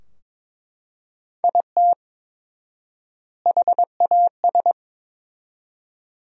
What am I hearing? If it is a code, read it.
Morse code, heard as a keyed tone, about 23 wpm, 700 Hz. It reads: IT HAS
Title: 4x Word-Spacing - Sentences from Top 500 Words - 22wpm